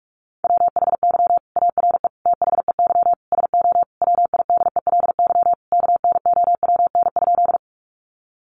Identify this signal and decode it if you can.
Morse code, heard as a keyed tone, about 44 wpm, 700 Hz. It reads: WHY ARE THEY SO WIDELY KNOWN?
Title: Sentences from Top 1200 Words 45wpm